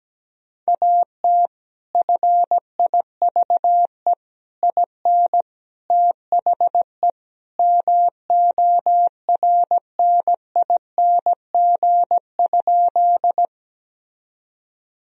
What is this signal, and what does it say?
Morse code, heard as a keyed tone, about 17 wpm, 700 Hz. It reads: AT FIVE IN THE MORNING?
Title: Sentences from Top 1000 Words - Review Part 2 - 17wpm